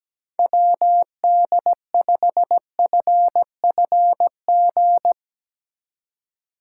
Morse code, heard as a keyed tone, about 17 wpm, 700 Hz. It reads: WD5FFG
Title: USA Call Signs - 6 Characters - 4x Character Spacing - 17wpm